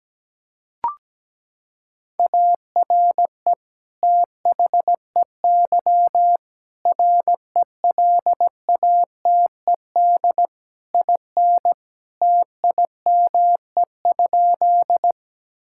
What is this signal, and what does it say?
Morse code, heard as a keyed tone, about 17 wpm, 700 Hz. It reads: ARE THEY RELATED IN TIME?
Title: Sentences from Top 1000 Words - Review Part 5 - 17wpm